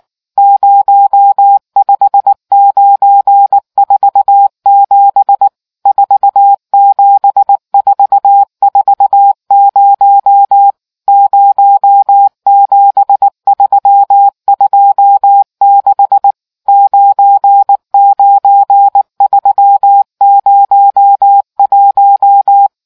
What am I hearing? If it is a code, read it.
Morse code, heard as a keyed tone, about 19 wpm, 800 Hz. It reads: 05947 47440 07326 99301